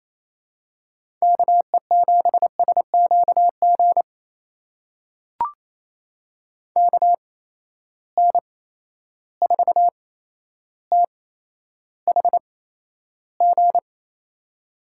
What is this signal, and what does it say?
Morse code, heard as a keyed tone, about 28 wpm, spaced as for 12 wpm, 700 Hz. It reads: KE7SQG KN4THG
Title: USA Call Signs - 6 Characters - 8x Character Spacing - 28wpm